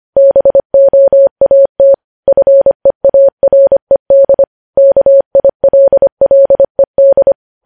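Morse code, heard as a keyed tone, about 25 wpm, 550 Hz. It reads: BOAT FEARED KILLED
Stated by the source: Morse code news headlines at 25 wpm.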